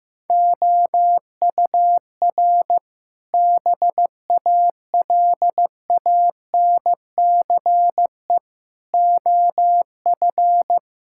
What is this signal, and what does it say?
Morse code, heard as a keyed tone, about 15 wpm, 700 Hz. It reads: OUR BALANCE OF
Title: Sentences from Top 1000 Words - Review Part 6 - 15wpm